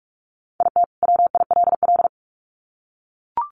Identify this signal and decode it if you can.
Morse code, heard as a keyed tone, about 44 wpm, 700 Hz. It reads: IT WILL
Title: Sets of 2 Words 45wpm